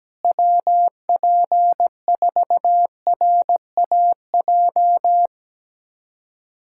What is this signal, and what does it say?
Morse code, heard as a keyed tone, about 17 wpm, 700 Hz. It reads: WP4RAJ
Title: USA Callsign DE Callsign - 6 Characters - 17wpm